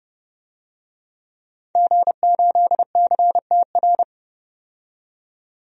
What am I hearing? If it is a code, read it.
Morse code, heard as a keyed tone, about 32 wpm, 700 Hz. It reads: G8CTR
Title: DXCC-100 Call Signs - 5 Characters - 30wpm